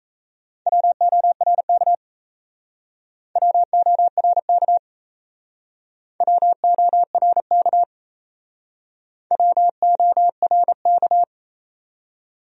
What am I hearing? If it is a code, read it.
Morse code, heard as a keyed tone, about 37 wpm, 700 Hz. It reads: WORK WORK WORK WORK